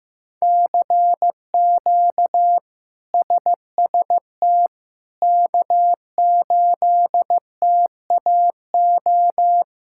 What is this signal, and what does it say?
Morse code, heard as a keyed tone, about 15 wpm, 700 Hz. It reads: CQ SST K8TAO
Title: K1USN SST QSO Practice - 15wpm